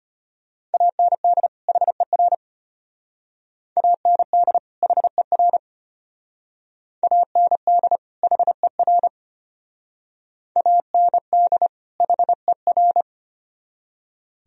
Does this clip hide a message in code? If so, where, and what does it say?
Yes: Morse code from 0.7 to 13.0 s, heard as a keyed tone, about 32 wpm, 700 Hz. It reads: AND HER AND HER AND HER AND HER